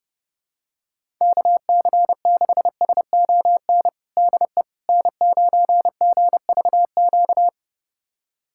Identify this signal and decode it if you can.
Morse code, heard as a keyed tone, about 32 wpm, 700 Hz. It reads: KC6SON DE N9GVQ
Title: USA Callsign DE Callsign 30wpm